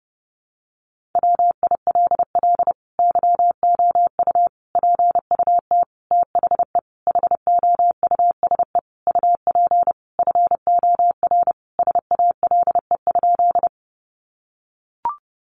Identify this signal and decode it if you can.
Morse code, heard as a keyed tone, about 30 wpm, 700 Hz. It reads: WILL YOU PUT THE HOUSE UP FOR SALE?
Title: Sentences from Top 1400 Words 30wpm